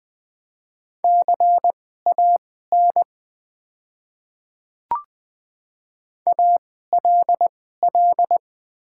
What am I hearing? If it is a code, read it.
Morse code, heard as a keyed tone, about 20 wpm, spaced as for 14 wpm, 700 Hz. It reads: CAN ALL
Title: 2x Character Spacing - Top 100 Words - 20wpm